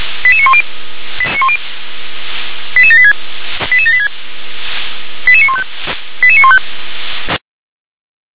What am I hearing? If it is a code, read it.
Five-tone selective call (ZVEI), heard as quick runs of five tones, tones of about 80 ms. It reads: 88910 88786 88786 88915 88915
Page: SDR/ZVEI